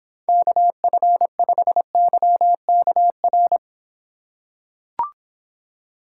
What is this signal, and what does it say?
Morse code, heard as a keyed tone, about 26 wpm, 700 Hz. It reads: KF5YKR